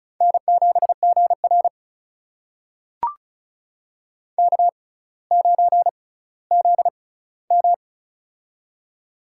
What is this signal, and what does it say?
Morse code, heard as a keyed tone, about 36 wpm, spaced as for 18 wpm, 700 Hz. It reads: N7GR K9ZM
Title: USA Call Signs - 4 Characters - 6x Character Spacing - 35wpm